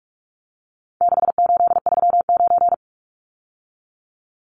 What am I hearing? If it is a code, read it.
Morse code, heard as a keyed tone, about 44 wpm, 700 Hz. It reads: 6839
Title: Numbers - 4-Digit 45wpm